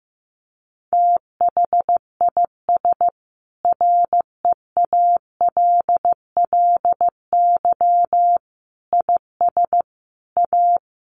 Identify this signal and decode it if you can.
Morse code, heard as a keyed tone, about 15 wpm, 700 Hz. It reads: THIS REALLY IS A
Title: Sentences from Top 700 Words 15wpm